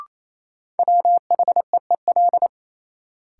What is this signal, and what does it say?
Morse code, heard as a keyed tone, about 28 wpm, 700 Hz. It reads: WHEEL
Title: Sentences from Top 500 Words - New Word Drill - 28wpm